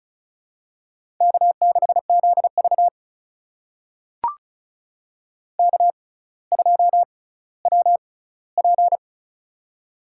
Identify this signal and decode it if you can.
Morse code, heard as a keyed tone, about 34 wpm, spaced as for 18 wpm, 700 Hz. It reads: K6ZV K2WP
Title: USA Call Signs - 4 Characters - 6x Character Spacing - 35wpm